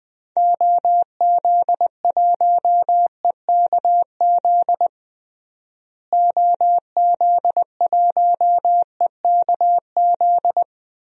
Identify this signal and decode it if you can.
Morse code, heard as a keyed tone, about 20 wpm, 700 Hz. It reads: OZ1EKZ OZ1EKZ
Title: DXCC-100 Call Signs - 6 Characters - Repeated 3x - 20wpm